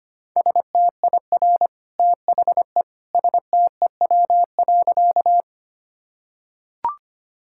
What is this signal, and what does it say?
Morse code, heard as a keyed tone, about 25 wpm, 700 Hz. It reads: STIR THE STEW.
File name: Easy Sentences with Hints - 25wpm